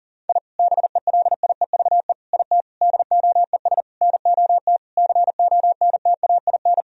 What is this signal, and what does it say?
Morse code, heard as a keyed tone, about 40 wpm, 700 Hz. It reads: I BELIEVE IT DOES NOT CONTAIN
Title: Sentences from Top 1000 Words - Review Part 2 - 40wpm